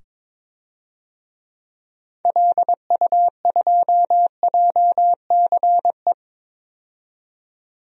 Morse code, heard as a keyed tone, about 21 wpm, 700 Hz. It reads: LU2JCE
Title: DXCC-100 Call Signs - 6 Characters - 22wpm